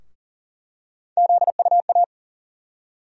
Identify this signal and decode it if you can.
Morse code, heard as a keyed tone, about 40 wpm, 700 Hz. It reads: ZUA
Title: USA Call Sign Postfixes - 3 Letters 40wpm